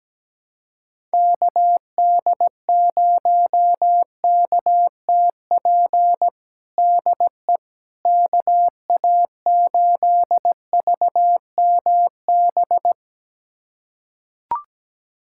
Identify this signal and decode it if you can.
Morse code, heard as a keyed tone, about 17 wpm, 700 Hz. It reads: KD0KTP DE KA8VMB
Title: USA Callsign DE Callsign 17wpm